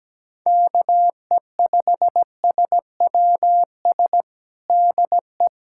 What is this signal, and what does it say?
Morse code, heard as a keyed tone, about 17 wpm, 700 Hz. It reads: KE5SWS DE
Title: USA Callsign DE Callsign - 6 Characters - 17wpm